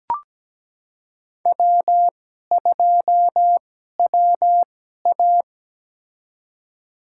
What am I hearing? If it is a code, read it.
Morse code, heard as a keyed tone, about 17 wpm, spaced as for 12 wpm, 700 Hz. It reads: W2WA
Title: USA Call Signs - 4 Characters - 2x Character Spacing - 17wpm